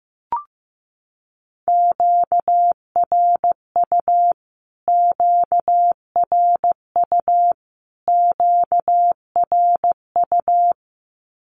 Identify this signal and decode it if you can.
Morse code, heard as a keyed tone, about 15 wpm, 700 Hz. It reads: QRU QRU QRU
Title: Q-Codes Repeated 3x 15wpm